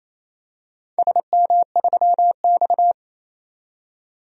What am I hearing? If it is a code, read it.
Morse code, heard as a keyed tone, about 28 wpm, 700 Hz. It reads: SM3X
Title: DXCC-100 Call Signs - 4 Characters - 28wpm